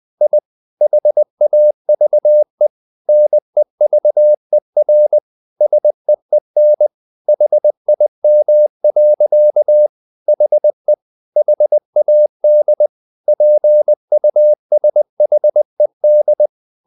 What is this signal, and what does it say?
Morse code, heard as a keyed tone, about 20 wpm, 600 Hz. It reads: I HAVE NEVER SEEN HIM. HE HAD PUSHED
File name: the old man and the sea / the old man and the sea part 1 20wpm